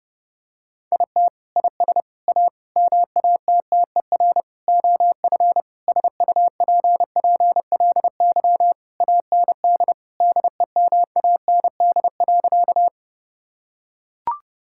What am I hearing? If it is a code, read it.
Morse code, heard as a keyed tone, about 29 wpm, 700 Hz. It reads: IT IS A MATTER OF SUPPLY AND DEMAND.
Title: Sentences from Top 1000 Words - Review Part 5 - 30wpm